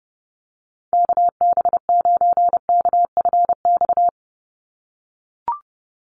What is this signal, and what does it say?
Morse code, heard as a keyed tone, about 30 wpm, 700 Hz. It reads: KB9KFX
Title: USA Call Signs 30wpm